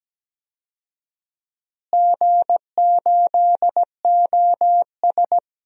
Morse code, heard as a keyed tone, about 17 wpm, 700 Hz. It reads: G8OS
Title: DXCC-100 Call Signs - 4 Characters - Repeated 2x - 17wpm